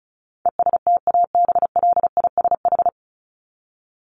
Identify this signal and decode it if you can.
Morse code, heard as a keyed tone, about 34 wpm, 700 Hz. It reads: ESTABLISH
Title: ED Suffix - Top 100 Words with Base Words 35wpm